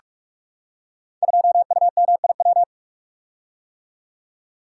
Morse code, heard as a keyed tone, about 44 wpm, 700 Hz. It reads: 2UMIW
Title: Code Groups - 5 Letter:Numbers - 45wpm